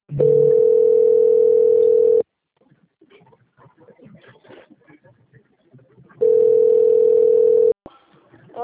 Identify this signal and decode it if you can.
Ringback tone (North America), 440 + 480 Hz.